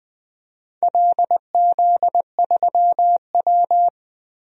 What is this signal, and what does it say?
Morse code, heard as a keyed tone, about 21 wpm, 700 Hz. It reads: LZ3W
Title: DXCC-100 Call Signs - 4 Characters - Repeated 3x - 20wpm